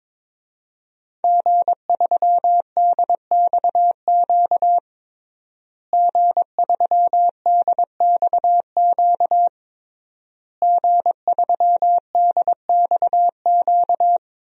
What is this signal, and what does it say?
Morse code, heard as a keyed tone, about 22 wpm, 700 Hz. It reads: G3DXQ G3DXQ G3DXQ
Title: DXCC-100 Call Signs - 5 Characters - Repeated 3x - 22wpm